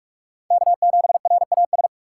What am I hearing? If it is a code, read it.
Morse code, heard as a keyed tone, about 45 wpm, 700 Hz. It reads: K7RAS
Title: USA Call Signs - 5 Characters - 6x Character Spacing - 45wpm